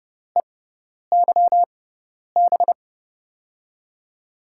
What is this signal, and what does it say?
Morse code, heard as a keyed tone, about 32 wpm, spaced as for 10 wpm, 700 Hz. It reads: EYB
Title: USA Call Signs - 5 Characters - 6x Character Spacing - 30wpm